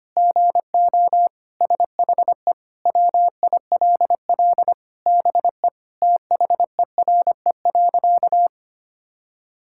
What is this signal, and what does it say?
Morse code, heard as a keyed tone, about 25 wpm, 700 Hz. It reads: GO SHE WILL BE THERE.
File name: practice / 25wpm / Sentences with 3-Word Phrases - 25wpm